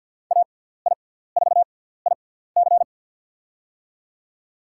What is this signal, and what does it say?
Morse code, heard as a keyed tone, about 48 wpm, spaced as for 17 wpm, 700 Hz. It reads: AI4IC
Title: USA Call Signs - 5 Characters - 6x Character Spacing - 50wpm